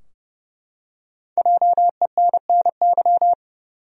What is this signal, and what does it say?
Morse code, heard as a keyed tone, about 31 wpm, 700 Hz. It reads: JENNY
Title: USA Common Names - Max 5 Letters - 30wpm